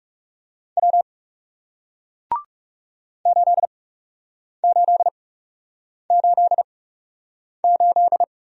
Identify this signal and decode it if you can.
Morse code, heard as a keyed tone, about 39 wpm, spaced as for 8 wpm, 700 Hz. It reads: W 8888